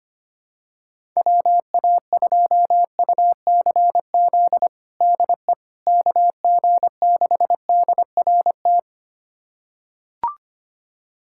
Morse code, heard as a keyed tone, about 25 wpm, 700 Hz. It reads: WA2UCZ DE KG6DRT
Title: USA Callsign DE Callsign - 6 Characters - 25wpm